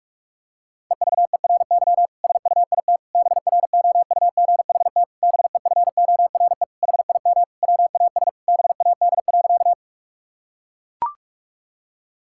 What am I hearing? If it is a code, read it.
Morse code, heard as a keyed tone, about 42 wpm, 700 Hz. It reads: EVERY SUIT BROUGHT BEFORE HIM WAS BAD.
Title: Sentences from Top 1600 Words 45wpm